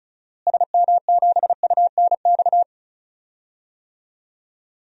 Morse code, heard as a keyed tone, about 36 wpm, 700 Hz. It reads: SM7UNX
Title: DXCC-100 Call Signs - 6 Characters - Repeated 3x - 35wpm